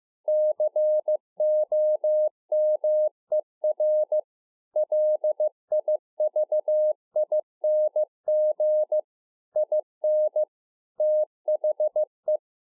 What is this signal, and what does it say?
Morse code, heard as a keyed tone, about 15 wpm, 610 Hz. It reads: COMER LIVING IN THE